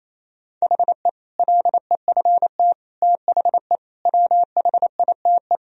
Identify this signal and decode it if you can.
Morse code, heard as a keyed tone, about 27 wpm, 700 Hz. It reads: HE LEFT THE WHITE
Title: Sentences from Top 1000 Words - Review Part 1 - 28wpm